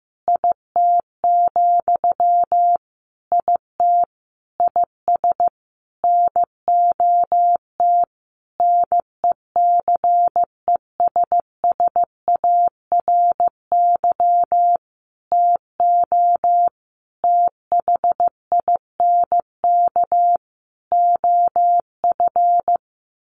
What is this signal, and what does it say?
Morse code, heard as a keyed tone, about 15 wpm, 700 Hz. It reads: IT, IT IS NOT NECESSARY TO THINK OF
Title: Sentences from Top 1300 Words 15wpm